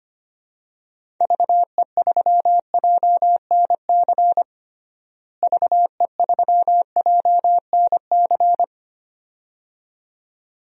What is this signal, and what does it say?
Morse code, heard as a keyed tone, about 25 wpm, 700 Hz. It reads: VE3JNC VE3JNC
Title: DXCC-100 Call Signs - 6 Characters - Repeated 2x - 25wpm